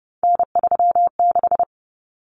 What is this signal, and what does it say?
Morse code, heard as a keyed tone, about 30 wpm, 700 Hz. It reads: N36
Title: Common Cut Numbers - 3-Digit 30wpm